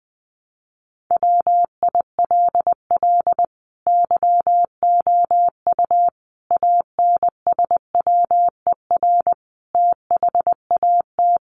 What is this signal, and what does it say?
Morse code, heard as a keyed tone, about 20 wpm, 700 Hz. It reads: WILL YOU ANSWER THAT
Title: Sentences from Top 400 Words 20wpm